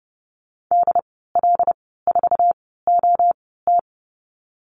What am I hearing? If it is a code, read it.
Morse code, heard as a keyed tone, about 30 wpm, spaced as for 17 wpm, 700 Hz. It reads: DL4OT
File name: Lesson 14g - Number 2 - Call Signs 30wpm